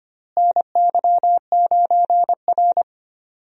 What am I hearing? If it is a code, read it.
Morse code, heard as a keyed tone, about 25 wpm, 700 Hz. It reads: NY9R
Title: USA Call Signs - 4 Characters - Repeated 3x - 25wpm